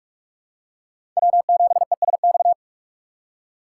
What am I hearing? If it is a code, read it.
Morse code, heard as a keyed tone, about 46 wpm, 700 Hz. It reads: W7ESX
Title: USA Call Signs - 5 Characters - 3x Character Spacing - 45wpm